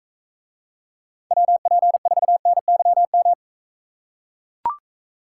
Morse code, heard as a keyed tone, about 42 wpm, 700 Hz. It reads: WP4NYM